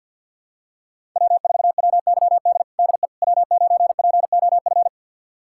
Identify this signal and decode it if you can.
Morse code, heard as a keyed tone, about 46 wpm, 700 Hz. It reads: W4WYD DE W9POF
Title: USA Callsign DE Callsign - 5 Characters - 50wpm